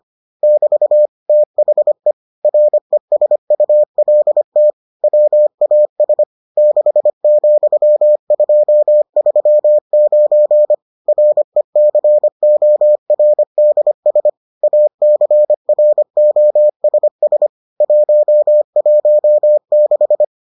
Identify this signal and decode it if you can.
Morse code, heard as a keyed tone, about 25 wpm, 600 Hz. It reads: <BT> THE RESULT WAS 6,239 RECORDS ACROSS 116